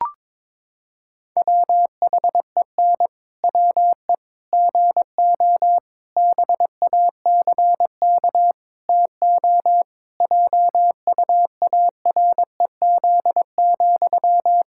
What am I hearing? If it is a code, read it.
Morse code, heard as a keyed tone, about 22 wpm, 700 Hz. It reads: WHEN WE GO BACK TO JUAREZ,
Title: Pangram Challenge v2 - 22wpm